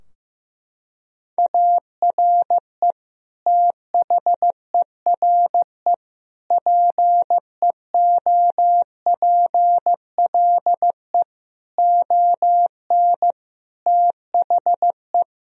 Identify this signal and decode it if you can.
Morse code, heard as a keyed tone, about 15 wpm, 700 Hz. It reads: ARE THERE PEOPLE ON THE
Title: Sentences from Top 1000 Words - Review Part 3 - 15wpm